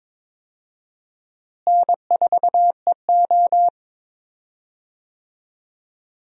Morse code, heard as a keyed tone, about 22 wpm, 700 Hz. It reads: N4EO
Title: USA Call Signs - 4 Characters - Repeated 2x - 22wpm